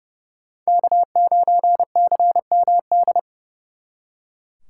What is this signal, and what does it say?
Morse code, heard as a keyed tone, about 39 wpm, 700 Hz. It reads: K9CMD